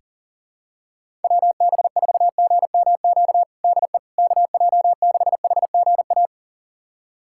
Morse code, heard as a keyed tone, about 40 wpm, 700 Hz. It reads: WB4GMQ DE KJ6HGA
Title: USA Callsign DE Callsign - 6 Characters - 40wpm